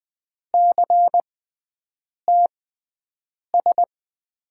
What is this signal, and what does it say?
Morse code, heard as a keyed tone, about 21 wpm, spaced as for 7 wpm, 700 Hz. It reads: CTS
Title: USA Call Signs - 6 Characters - 6x Character Spacing - 20wpm